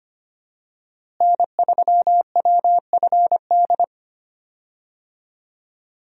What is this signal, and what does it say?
Morse code, heard as a keyed tone, about 25 wpm, 700 Hz. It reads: N3WFD
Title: USA Call Signs - 5 Characters - Repeated 2x - 25wpm